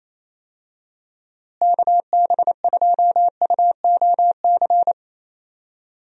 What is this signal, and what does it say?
Morse code, heard as a keyed tone, about 28 wpm, 700 Hz. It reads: KB2UOC
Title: USA Call Signs - 6 Characters - Repeated 2x - 28wpm